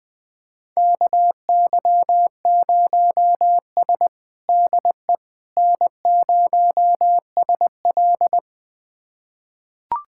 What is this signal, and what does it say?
Morse code, heard as a keyed tone, about 20 wpm, 700 Hz. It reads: KY0S DE N0SL
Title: USA Callsign DE Callsign - 4 Characters - 20wpm